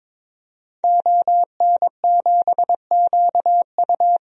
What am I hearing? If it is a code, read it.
Morse code, heard as a keyed tone, about 22 wpm, 700 Hz. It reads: ON7QU
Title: DXCC-100 Call Signs - 5 Characters - 22wpm